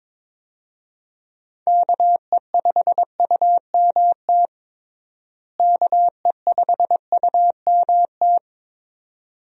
Morse code, heard as a keyed tone, about 22 wpm, 700 Hz. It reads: KE5UMT KE5UMT
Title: USA Call Signs - 6 Characters - Repeated 2x - 22wpm